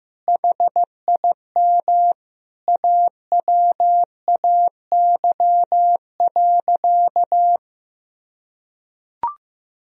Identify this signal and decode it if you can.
Morse code, heard as a keyed tone, about 15 wpm, 700 Hz. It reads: HIM AWAY.
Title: Sentences from Top 1000 Words - Review Part 4 - 15wpm